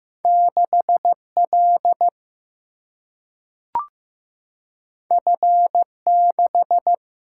Morse code, heard as a keyed tone, about 15 wpm, 700 Hz. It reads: 6L F6
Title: Single-Letter Single-Number 15wpm